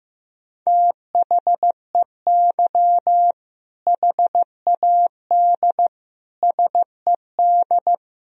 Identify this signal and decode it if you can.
Morse code, heard as a keyed tone, about 15 wpm, 700 Hz. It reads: THEY HAD SED
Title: Sentences from Top 400 Words and Abbreviations 15wpm